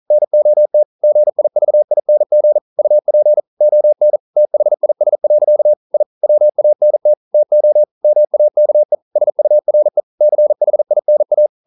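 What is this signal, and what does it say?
Morse code, heard as a keyed tone, about 41 wpm, 600 Hz. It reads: NOT GIVING UP ON THIS. I WANT TO MAKE SURE CHINA
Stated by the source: Morse code practice at 41 wpm, from CW med Gnister.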